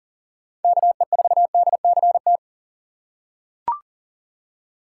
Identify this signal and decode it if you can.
Morse code, heard as a keyed tone, about 40 wpm, 700 Hz. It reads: KE4DCT
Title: USA Call Signs - 6 Characters - 6x Character Spacing - 40wpm